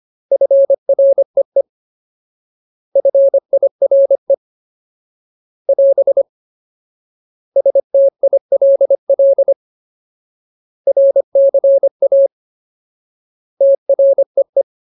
Morse code, heard as a keyed tone, about 25 wpm, 550 Hz. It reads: FREE FIRE <AS> STILL RCA TREE
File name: Ultimate Word List / 25WPM Ultimate